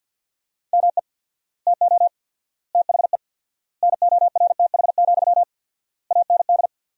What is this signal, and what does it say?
Morse code, heard as a keyed tone, about 50 wpm, 700 Hz. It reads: ME TO THE NORTH, AND
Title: 4x Word-Spacing - Sentences from Top 400 Words - 50wpm